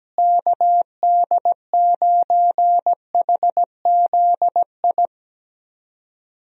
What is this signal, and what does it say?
Morse code, heard as a keyed tone, about 17 wpm, 700 Hz. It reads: KD9HZI
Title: USA Call Signs - 6 Characters - Repeated 2x - 17wpm